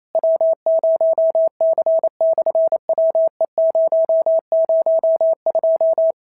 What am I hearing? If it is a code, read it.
Morse code, heard as a keyed tone, about 28 wpm, 650 Hz. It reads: W0C/WE002
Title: SOTA QSO Practice 28wpm